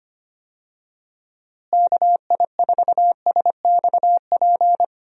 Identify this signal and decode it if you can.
Morse code, heard as a keyed tone, about 25 wpm, 700 Hz. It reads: KI4SXP
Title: USA Call Signs - 6 Characters - Repeated 3x - 25wpm